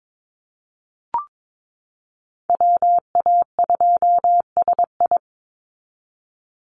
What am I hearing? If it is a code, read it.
Morse code, heard as a keyed tone, about 22 wpm, 700 Hz. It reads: WA2SI
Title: Call Signs Repeated 3x - 22wpm